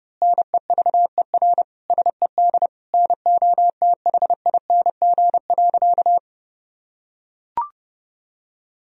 Morse code, heard as a keyed tone, about 30 wpm, 700 Hz. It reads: NEVER SED NOTHING.
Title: Sentences from Top 500 Words and Abbreviations 30wpm